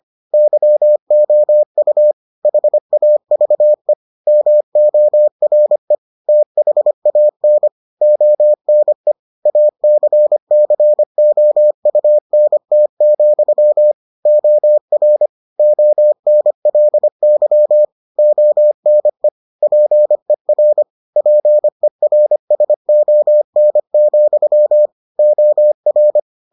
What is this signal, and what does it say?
Morse code, heard as a keyed tone, about 25 wpm, 600 Hz. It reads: YOU HAVE MORE THAN ONE ACCOUNT, OR ONLY ONE PER PERSON, OR